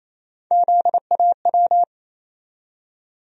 Morse code, heard as a keyed tone, about 28 wpm, 700 Hz. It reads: ZAW
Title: USA Call Sign Postfixes - 3 Letters 28wpm